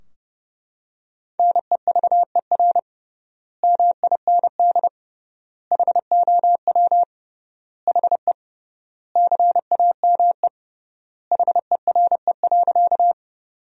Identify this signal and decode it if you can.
Morse code, heard as a keyed tone, about 31 wpm, 700 Hz. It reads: NEVER MIND HOW HE CAME HERE.
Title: 3x Word-Spacing - Sentences from Top 500 Words - 30wpm